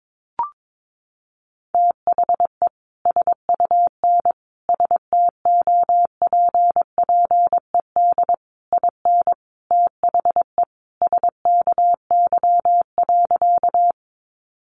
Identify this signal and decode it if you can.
Morse code, heard as a keyed tone, about 22 wpm, 700 Hz. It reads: THE SUN STOPPED IN THE SKY.
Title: A Sound Of Thunder 22wpm